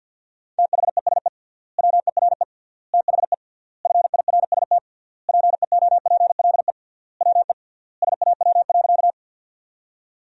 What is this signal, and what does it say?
Morse code, heard as a keyed tone, about 50 wpm, 700 Hz. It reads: THESE WERE THE FIRST PEOPLE WE SAW.
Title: 3x Word-Spacing - Sentences from Top 300 Words - 50wpm